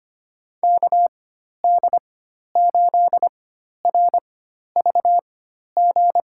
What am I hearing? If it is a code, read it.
Morse code, heard as a keyed tone, about 25 wpm, spaced as for 12 wpm, 700 Hz. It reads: KD8RVG
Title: USA Call Signs - 4x Character Spacing - 25wpm